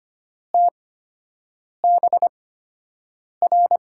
Morse code, heard as a keyed tone, about 25 wpm, spaced as for 7 wpm, 700 Hz. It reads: TBR
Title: USA Call Signs - 5 Characters - 8x Character Spacing - 25wpm